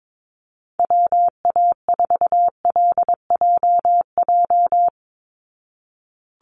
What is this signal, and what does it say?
Morse code, heard as a keyed tone, about 22 wpm, 700 Hz. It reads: WA4LJJ
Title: Call Signs Repeated 3x - 22wpm